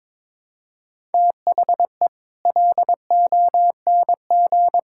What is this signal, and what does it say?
Morse code, heard as a keyed tone, about 22 wpm, 700 Hz. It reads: THE LONG
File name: Sets of 9 Words - 22wpm